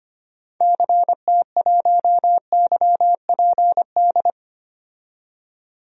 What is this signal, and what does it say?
Morse code, heard as a keyed tone, about 25 wpm, 700 Hz. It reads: CT1YPD
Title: DXCC-100 Call Signs - 6 Characters - 25wpm